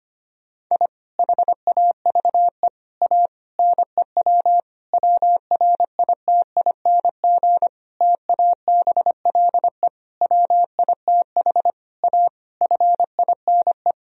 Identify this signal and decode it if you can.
Morse code, heard as a keyed tone, about 25 wpm, 700 Hz. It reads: I HAVE A NEW WRITING TABLE WITH A FINE
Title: Sentences from Top 1000 Words - Review Part 3 - 25wpm